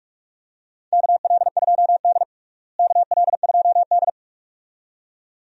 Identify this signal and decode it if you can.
Morse code, heard as a keyed tone, about 45 wpm, 700 Hz. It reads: KL2D KL2D
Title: USA Call Signs - 4 Characters - Repeated 2x - 45wpm